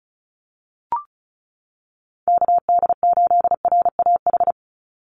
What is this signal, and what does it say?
Morse code, heard as a keyed tone, about 34 wpm, 700 Hz. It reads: KD8RAH
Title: USA Call Signs 35wpm